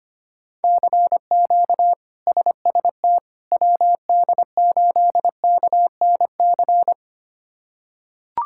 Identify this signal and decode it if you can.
Morse code, heard as a keyed tone, about 25 wpm, 700 Hz. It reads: CQ SST WD8KNC
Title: K1USN SST QSO Practice - 25wpm